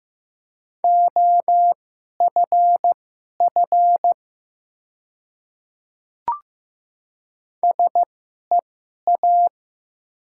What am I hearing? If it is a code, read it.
Morse code, heard as a keyed tone, about 15 wpm, spaced as for 11 wpm, 700 Hz. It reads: OFF SEA
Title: 2x Character Spacing - Top 300 Words - 15wpm